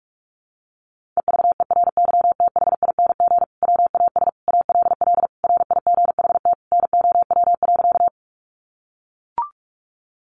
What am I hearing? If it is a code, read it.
Morse code, heard as a keyed tone, about 44 wpm, 700 Hz. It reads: EVERYTHING WAS ALL RIGHT NOW.
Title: Sentences from Top 700 Words 45wpm